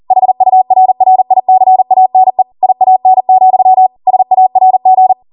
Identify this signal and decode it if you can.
Morse code, heard as a keyed tone, about 40 wpm, 750 Hz. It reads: HURRICANE IAN, SARG